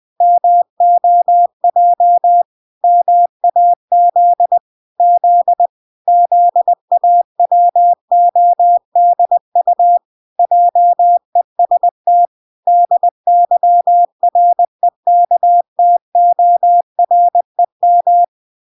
Morse code, heard as a keyed tone, about 20 wpm, 700 Hz. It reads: MOJ MAZ Z ZAWODU JEST DYREKTOREM